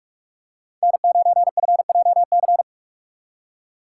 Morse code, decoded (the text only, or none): N9FJC